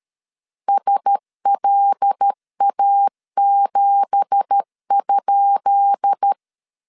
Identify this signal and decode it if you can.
Morse code, heard as a keyed tone, about 13 wpm, 800 Hz. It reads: SLA7?
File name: LEZIONI / 10 SIGNS / 10D COMA 12wpm 2544